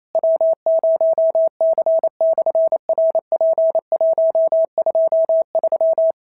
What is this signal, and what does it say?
Morse code, heard as a keyed tone, about 28 wpm, 650 Hz. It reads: W0C/RP123
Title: SOTA QSO Practice 28wpm